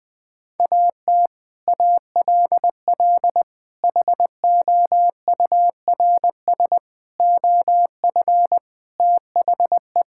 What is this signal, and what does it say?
Morse code, heard as a keyed tone, about 20 wpm, 700 Hz. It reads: AT ALL HOURS OF THE
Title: Sets of 8 Words - 20wpm